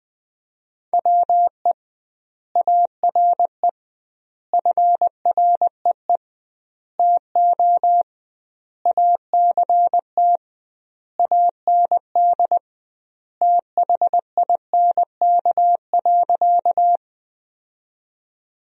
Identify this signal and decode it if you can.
Morse code, heard as a keyed tone, about 20 wpm, 700 Hz. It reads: WE ARE FREE TO ACT AND THINK.
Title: 2x Word-Spacing - Sentences from Top 500 Words - 20wpm